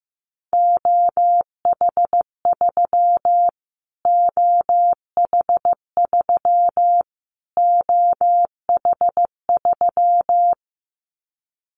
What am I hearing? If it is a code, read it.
Morse code, heard as a keyed tone, about 15 wpm, 700 Hz. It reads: OH3 OH3 OH3
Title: Callsign Prefixes for Top 25 DXCC - Repeated 3x - 15wpm